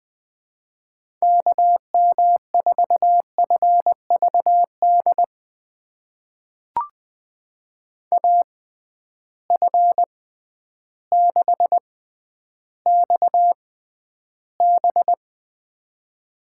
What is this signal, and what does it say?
Morse code, heard as a keyed tone, about 21 wpm, spaced as for 11 wpm, 700 Hz. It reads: KM4FVD AF6XB
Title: USA Call Signs - 6x Character Spacing - 20wpm